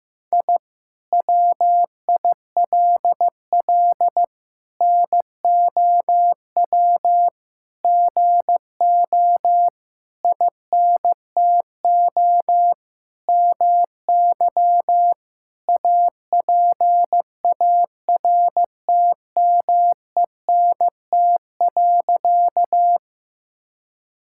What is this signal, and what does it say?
Morse code, heard as a keyed tone, about 15 wpm, 700 Hz. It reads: I WILL NOW GO INTO MY APARTMENT.
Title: Sentences from Top 1000 Words - Review Part 6 - 15wpm